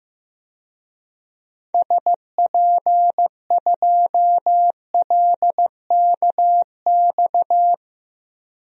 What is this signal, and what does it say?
Morse code, heard as a keyed tone, about 15 wpm, 700 Hz. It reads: SP2LKX
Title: DXCC-100 Call Signs - 6 Characters - Repeated 3x - 15wpm